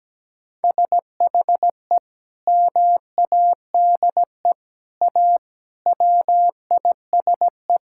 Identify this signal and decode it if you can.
Morse code, heard as a keyed tone, about 17 wpm, 700 Hz. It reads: SHE MADE A WISE
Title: Sentences from Top 1600 Words 17wpm